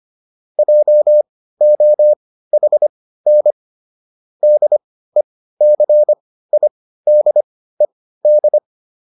Morse code, heard as a keyed tone, about 25 wpm, spaced as for 15 wpm, 600 Hz. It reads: JOHN DECIDED